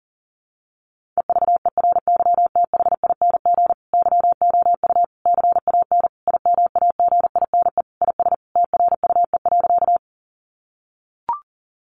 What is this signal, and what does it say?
Morse code, heard as a keyed tone, about 40 wpm, 700 Hz. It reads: EVERYTHING YOU CAN IMAGINE IS TRUE.
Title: Sentences from Top 1400 Words 40wpm